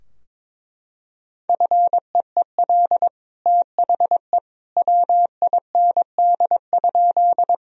Morse code, heard as a keyed tone, about 21 wpm, 700 Hz. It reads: FEEL THE WIND?
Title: Sentences from Top 1000 Words - Review Part 2 - 22wpm